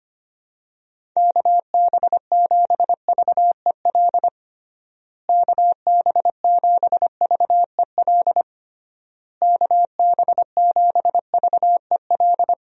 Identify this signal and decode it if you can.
Morse code, heard as a keyed tone, about 25 wpm, 700 Hz. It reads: KB7VEL KB7VEL KB7VEL
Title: USA Call Signs - 6 Characters - Repeated 3x - 25wpm